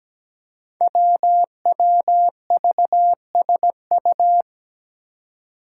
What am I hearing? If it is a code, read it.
Morse code, heard as a keyed tone, about 17 wpm, 700 Hz. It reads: WWVSU
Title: Five Challenging Letter-Number Combinations 17wpm